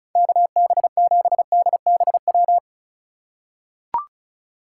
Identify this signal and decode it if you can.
Morse code, heard as a keyed tone, about 36 wpm, 700 Hz. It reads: KB7DBW
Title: USA Call Signs - 2x Character Spacing - 35wpm